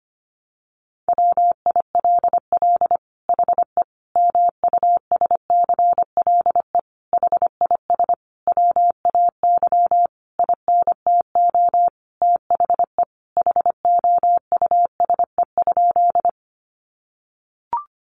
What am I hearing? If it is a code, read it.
Morse code, heard as a keyed tone, about 25 wpm, 700 Hz. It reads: WILL HE MUSCLE HIS WAY INTO THE HOUSE?
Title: Sentences from Top 1300 Words 25wpm